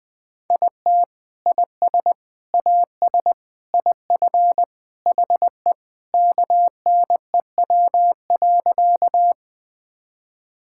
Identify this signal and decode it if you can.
Morse code, heard as a keyed tone, about 20 wpm, 700 Hz. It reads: IT IS AS IF HE KNEW.
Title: Sentences with 3-Word Phrases - Spoken Beginning - 20wpm